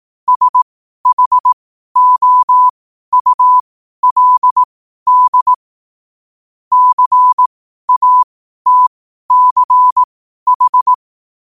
Morse code, heard as a keyed tone, about 18 wpm, spaced as for 12 wpm, 1000 Hz. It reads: SHOULD CATCH
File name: qotdfiles / 12wpm